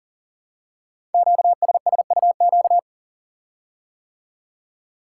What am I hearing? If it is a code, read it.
Morse code, heard as a keyed tone, about 41 wpm, 700 Hz. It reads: QSSUQ
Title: Code Groups - 5 Letters - 40wpm